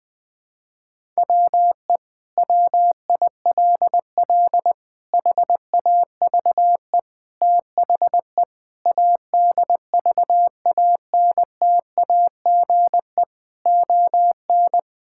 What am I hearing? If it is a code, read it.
Morse code, heard as a keyed tone, about 20 wpm, 700 Hz. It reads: WE WILL HAVE THE ADVANTAGE ON
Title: Sentences from Top 1000 Words - Review Part 6 - 20wpm